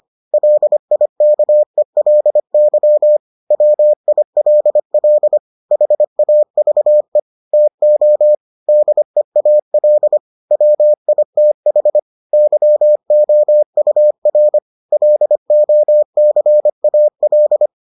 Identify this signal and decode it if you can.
Morse code, heard as a keyed tone, about 25 wpm, 600 Hz. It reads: LIKELY WILL HAVE TO DEAL WITH YOUR LOCAL